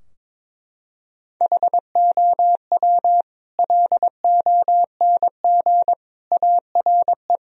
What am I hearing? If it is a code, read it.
Morse code, heard as a keyed tone, about 22 wpm, 700 Hz. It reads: HOW LONG ARE
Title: Sentences from Top 1000 Words - Review Part 2 - 22wpm